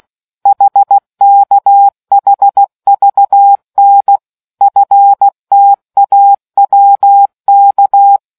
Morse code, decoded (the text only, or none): HKHVN FTAWK